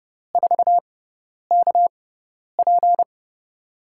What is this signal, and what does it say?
Morse code, heard as a keyed tone, about 29 wpm, spaced as for 10 wpm, 700 Hz. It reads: 4KP
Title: USA Call Signs - 4 Characters - 6x Character Spacing - 30wpm